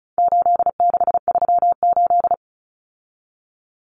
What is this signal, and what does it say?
Morse code, heard as a keyed tone, about 34 wpm, 700 Hz. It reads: 8638
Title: Common Cut Numbers - 4-Digit 35wpm